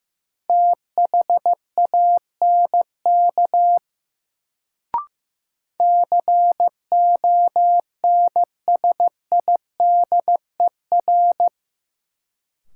Morse code, heard as a keyed tone, about 15 wpm, 700 Hz. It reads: THANK CONSIDER
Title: Sentences from Top 600 Words - New Word Drill - 15wpm